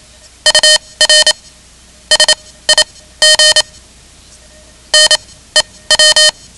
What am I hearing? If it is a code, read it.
Morse code, heard as a keyed tone, about 27 wpm, spaced as for 17 wpm, 3050 Hz. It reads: UR SIG NEW
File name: ham radio / CodeQuick / Code Quick 10